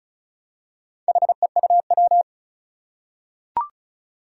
Morse code, heard as a keyed tone, about 36 wpm, 700 Hz. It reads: HEUW
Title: Four Challenging Letter-Number Combinations 35wpm